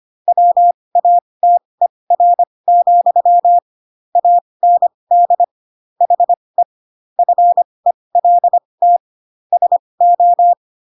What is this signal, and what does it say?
Morse code, heard as a keyed tone, about 25 wpm, spaced as for 20 wpm, 700 Hz. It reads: WATER, AND HE FELT SO
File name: Pinocchio / Pinocchio ch8 15 25 ik1hge